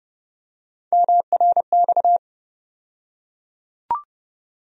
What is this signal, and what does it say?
Morse code, heard as a keyed tone, about 30 wpm, 700 Hz. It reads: MRX
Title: USA Call Sign Postfixes - 3 Letters 30wpm